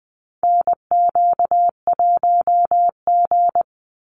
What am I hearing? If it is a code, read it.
Morse code, heard as a keyed tone, about 20 wpm, 700 Hz. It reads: NQ1G
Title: USA Call Signs 20wpm